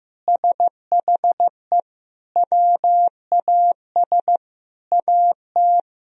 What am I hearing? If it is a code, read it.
Morse code, heard as a keyed tone, about 15 wpm, 700 Hz. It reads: SHE WAS AT
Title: Sentences from Top 1000 Words - Review Part 1 - 15wpm